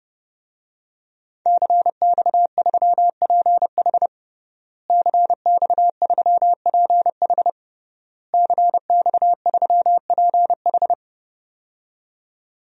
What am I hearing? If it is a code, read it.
Morse code, heard as a keyed tone, about 30 wpm, 700 Hz. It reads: CX3PH CX3PH CX3PH
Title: DXCC-100 Call Signs - 5 Characters - Repeated 3x - 30wpm